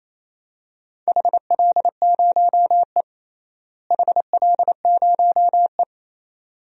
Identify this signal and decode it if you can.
Morse code, heard as a keyed tone, about 27 wpm, 700 Hz. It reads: HL0E HL0E
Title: DXCC-100 Call Signs - 4 Characters - Repeated 2x - 28wpm